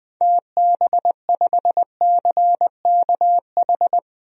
Morse code, heard as a keyed tone, about 21 wpm, 700 Hz. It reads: TB5CKH
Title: DXCC-100 Call Signs - 6 Characters - 20wpm